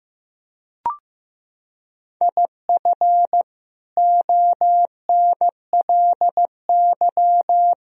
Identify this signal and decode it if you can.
Morse code, heard as a keyed tone, about 15 wpm, 700 Hz. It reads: IF ONLY
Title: Sentences from Top 500 Words and Abbreviations 15wpm